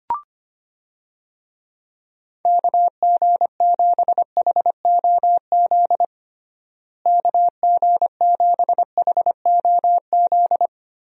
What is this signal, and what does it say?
Morse code, heard as a keyed tone, about 25 wpm, 700 Hz. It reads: KG7HOZ KG7HOZ
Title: USA Call Signs - 6 Characters - Repeated 3x - 25wpm